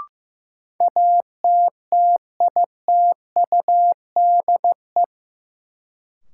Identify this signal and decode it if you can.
Morse code, heard as a keyed tone, about 15 wpm, 700 Hz. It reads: ATTITUDE
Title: Sentences from Top 1300 Words - New Word Drill - 15wpm